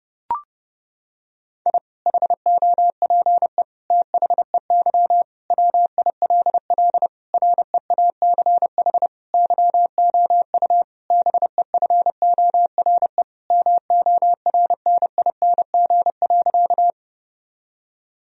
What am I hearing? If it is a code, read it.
Morse code, heard as a keyed tone, about 30 wpm, 700 Hz. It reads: I HOPE THEY WILL REACH YOU BEFORE MORNING.
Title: Sentences from Top 1000 Words - Review Part 2 - 30wpm